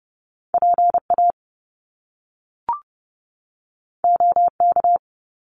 Morse code, heard as a keyed tone, about 30 wpm, 700 Hz. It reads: PA OK
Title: Top Two Letter Combinations in Words 30wpm